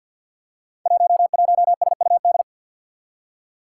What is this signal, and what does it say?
Morse code, heard as a keyed tone, about 53 wpm, 700 Hz. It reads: 11SUD
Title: Five Challenging Letter-Number Combinations 50wpm